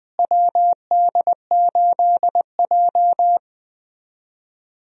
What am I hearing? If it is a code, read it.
Morse code, heard as a keyed tone, about 21 wpm, 700 Hz. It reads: WD8J
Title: USA Call Signs - 4 Characters - Repeated 3x - 20wpm